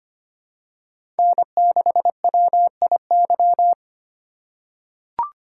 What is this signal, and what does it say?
Morse code, heard as a keyed tone, about 25 wpm, 700 Hz. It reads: N6WIY
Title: USA Call Signs - 4x Character Spacing - 25wpm